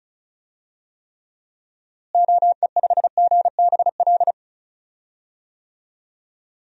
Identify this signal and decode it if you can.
Morse code, heard as a keyed tone, about 36 wpm, 700 Hz. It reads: OE5GBL
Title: DXCC-100 Call Signs - 6 Characters - Repeated 2x - 35wpm